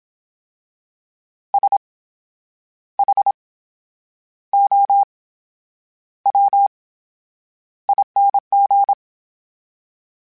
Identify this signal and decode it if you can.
Morse code, heard as a keyed tone, about 27 wpm, spaced as for 9 wpm, 800 Hz. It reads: SHOWING
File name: Special INGs / Special INGs-mp3s / VerySlowToFaster Ing @25